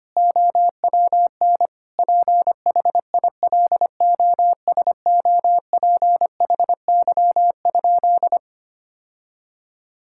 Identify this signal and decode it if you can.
Morse code, heard as a keyed tone, about 25 wpm, 700 Hz. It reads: OWN PHILOSOPHY?
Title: Sentences from Top 1000 Words - Review Part 6 - 25wpm